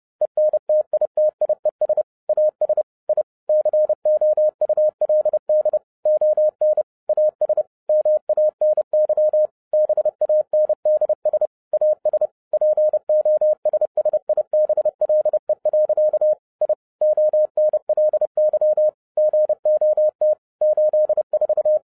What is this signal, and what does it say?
Morse code, heard as a keyed tone, about 30 wpm, 610 Hz. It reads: ENTITIES AS I COULD ON AS MANY BANDS AS POSSIBLE. I ONLY GOT 84
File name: 2012-01-mp3 / n11-I DIDNT MISS THE SOLAR CYCLE 24-30wpm